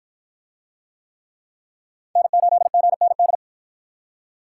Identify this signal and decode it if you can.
Morse code, heard as a keyed tone, about 53 wpm, 700 Hz. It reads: N8GND